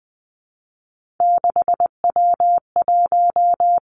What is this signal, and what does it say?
Morse code, heard as a keyed tone, about 20 wpm, 700 Hz. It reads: 6W1
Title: Three Challenging Letter-Number Combinations 20wpm